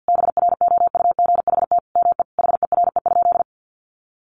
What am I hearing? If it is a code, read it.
Morse code, heard as a keyed tone, about 48 wpm, 700 Hz. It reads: BROUGHT ME HERE?
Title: Sentences from Top 1200 Words 50wpm